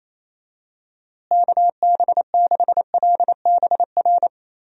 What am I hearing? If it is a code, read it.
Morse code, heard as a keyed tone, about 28 wpm, 700 Hz. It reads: KB6LBR
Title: USA Call Signs - 6 Characters - Repeated 2x - 28wpm